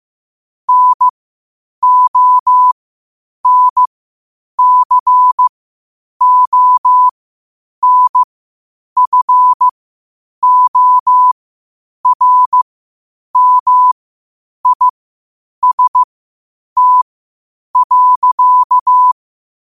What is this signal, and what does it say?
Morse code, heard as a keyed tone, about 15 wpm, spaced as for 8 wpm, 1000 Hz. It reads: NONCONFORMIST.
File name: qotdfiles / 8wpm